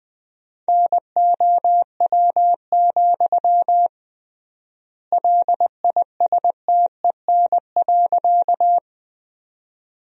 Morse code, heard as a keyed tone, about 20 wpm, 700 Hz. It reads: NOW, LISTEN.
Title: 3x Word-Spacing - Sentences from Top 400 Words - 20wpm